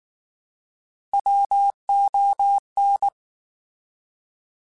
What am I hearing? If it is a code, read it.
Morse code, heard as a keyed tone, about 19 wpm, 800 Hz. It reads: WON
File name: Alphabetical / Alphabetical mp3s / Ws @18